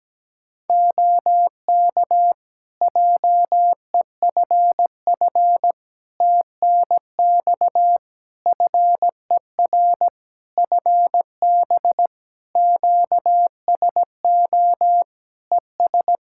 Morse code, heard as a keyed tone, about 17 wpm, 700 Hz. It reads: OK JEFF TNX FER FB QSO ES
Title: K1USN SST QSO Practice 17wpm